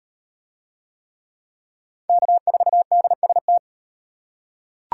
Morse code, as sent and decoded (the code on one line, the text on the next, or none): -.- ....- -.. ... -
K4DST